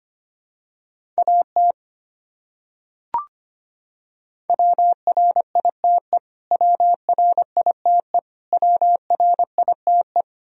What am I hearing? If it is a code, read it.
Morse code, heard as a keyed tone, about 25 wpm, 700 Hz. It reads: AT WRITE WRITE WRITE
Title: Top 100 Words Repeated 25wpm